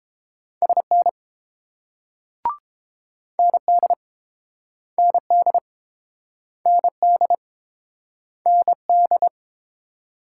Morse code, heard as a keyed tone, about 28 wpm, 700 Hz. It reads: SN ND ND ND ND